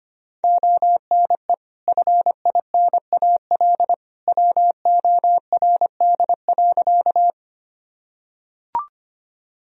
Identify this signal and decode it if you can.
Morse code, heard as a keyed tone, about 25 wpm, 700 Hz. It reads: ONE FINAL WORD.
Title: Sentences from Top 1000 Words - Review Part 2 - 25wpm